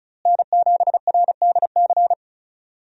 Morse code, heard as a keyed tone, about 34 wpm, 700 Hz. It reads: N7RDC
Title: USA Call Signs - 5 Characters - Repeated 2x - 35wpm